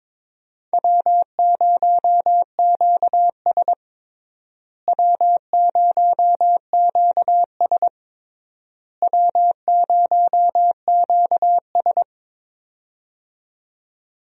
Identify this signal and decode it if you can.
Morse code, heard as a keyed tone, about 22 wpm, 700 Hz. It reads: W0QS W0QS W0QS
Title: USA Call Signs - 4 Characters - Repeated 3x - 22wpm